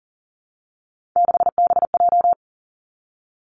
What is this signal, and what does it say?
Morse code, heard as a keyed tone, about 42 wpm, 700 Hz. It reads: 6BJ
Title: Three Challenging Letter-Number Combinations 40wpm